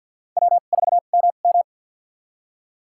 Morse code, heard as a keyed tone, about 53 wpm, spaced as for 37 wpm, 700 Hz. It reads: W4MM
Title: USA Call Signs - 4 Characters - 2x Character Spacing - 50wpm